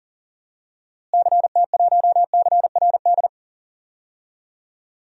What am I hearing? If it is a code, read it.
Morse code, heard as a keyed tone, about 40 wpm, 700 Hz. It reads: CT1CRD
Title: DXCC-100 Call Signs - 6 Characters - Repeated 2x - 40wpm